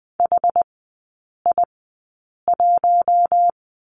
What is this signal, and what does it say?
Morse code, heard as a keyed tone, about 20 wpm, spaced as for 8 wpm, 690 Hz. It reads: HI1
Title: 081 Lesson 14e - Number 2 - Three Characters 20wpm